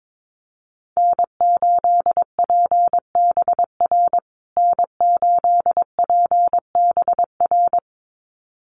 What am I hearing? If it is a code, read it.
Morse code, heard as a keyed tone, about 22 wpm, 700 Hz. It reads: N8PBR N8PBR
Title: USA Call Signs Repeated 2x 22wpm